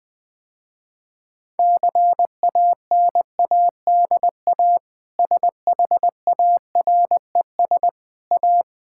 Morse code, heard as a keyed tone, about 21 wpm, 700 Hz. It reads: CANADA SHARES A
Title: Sentences with 4-Word Phrases - Spoken Beginning - 20wpm